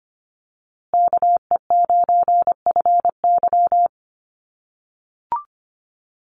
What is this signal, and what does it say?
Morse code, heard as a keyed tone, about 25 wpm, 700 Hz. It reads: KE9FY
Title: USA Call Signs Repeated 2x 25wpm